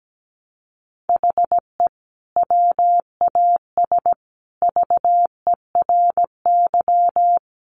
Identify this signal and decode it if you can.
Morse code, heard as a keyed tone, about 17 wpm, 700 Hz. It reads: HE WAS VERY
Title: Sentences from Top 1500 Words 17wpm